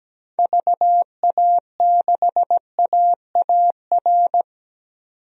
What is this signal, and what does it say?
Morse code, heard as a keyed tone, about 17 wpm, 700 Hz. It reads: VA6AAR
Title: DXCC-100 Call Signs - 6 Characters - 17wpm